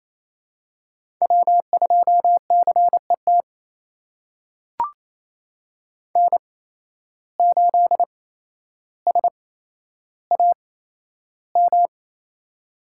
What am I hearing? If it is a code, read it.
Morse code, heard as a keyed tone, about 28 wpm, spaced as for 12 wpm, 700 Hz. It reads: W2CET N8SAM
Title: USA Call Signs - 5 Characters - 8x Character Spacing - 28wpm